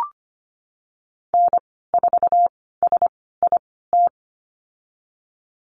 Morse code, heard as a keyed tone, about 25 wpm, spaced as for 16 wpm, 700 Hz. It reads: N4SIT
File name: Lesson 15g - Number 5 - Call Signs 25wpm